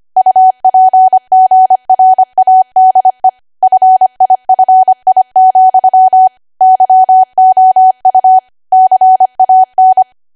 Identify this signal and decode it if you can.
Morse code, heard as a keyed tone, about 25 wpm, 750 Hz. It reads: UPGRADE FIFI, YOU CAN
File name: cwa / W1AW-Level3 / 25WPM